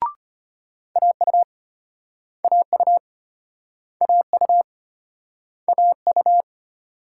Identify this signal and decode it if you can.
Morse code, heard as a keyed tone, about 35 wpm, 700 Hz. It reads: AU AU AU AU